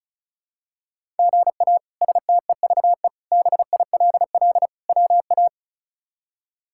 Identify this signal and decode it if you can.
Morse code, heard as a keyed tone, about 36 wpm, 700 Hz. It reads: GA STEVE BILL WA
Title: K1USN SST QSO Practice - 35wpm